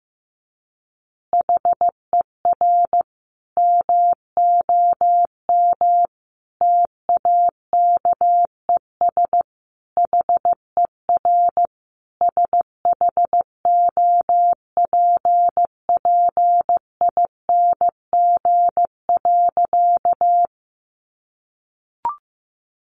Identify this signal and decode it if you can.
Morse code, heard as a keyed tone, about 15 wpm, 700 Hz. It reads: HER MOM TAKES HER SHOPPING.
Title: Additional Easy Phrases 15wpm